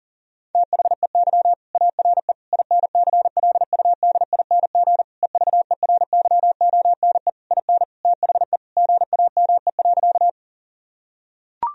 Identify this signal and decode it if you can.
Morse code, heard as a keyed tone, about 40 wpm, 700 Hz. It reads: THEY ARE INCLUDING EVERYONE IN THE GAME.
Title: Sentences from Top 1000 Words - Review Part 4 - 40wpm